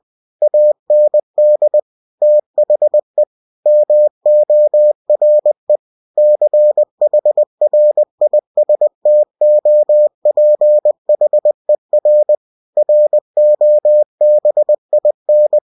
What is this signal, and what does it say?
Morse code, heard as a keyed tone, about 20 wpm, 600 Hz. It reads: AND THE MORE CHRISTOPHER ROBIN
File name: cw-books / winnie-the-pooh / winnie-20wpm-page-0099